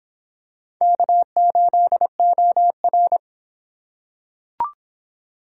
Morse code, heard as a keyed tone, about 24 wpm, 700 Hz. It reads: K8OR